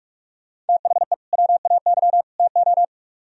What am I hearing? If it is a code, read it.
Morse code, heard as a keyed tone, about 46 wpm, 700 Hz. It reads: THE WAY TO
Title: Speed-Racing - Sets of 3 Words 45wpm